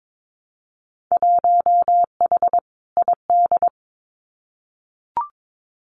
Morse code, heard as a keyed tone, about 22 wpm, 700 Hz. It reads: 1H ID
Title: Winter Field Day Exchanges 22wpm